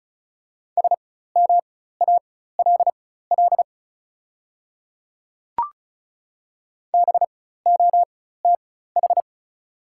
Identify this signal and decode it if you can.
Morse code, heard as a keyed tone, about 36 wpm, spaced as for 16 wpm, 690 Hz. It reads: SMALL BOTH
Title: 4x Character Spacing - Top 300 Words - 35wpm